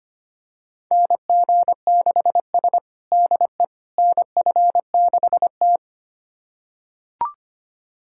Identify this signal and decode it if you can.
Morse code, heard as a keyed tone, about 25 wpm, 700 Hz. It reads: NG6S DE NF6T
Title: USA Callsign DE Callsign - 4 Characters - 25wpm